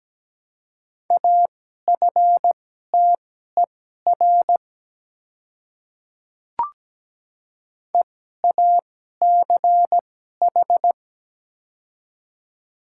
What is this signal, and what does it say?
Morse code, heard as a keyed tone, about 17 wpm, spaced as for 12 wpm, 700 Hz. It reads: AFTER EACH
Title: 2x Character Spacing - Top 200 Words - 17wpm